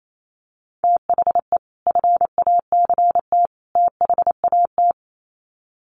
Morse code, heard as a keyed tone, about 28 wpm, 700 Hz. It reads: THE FACT THAT
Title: Sets of 3 Words 28wpm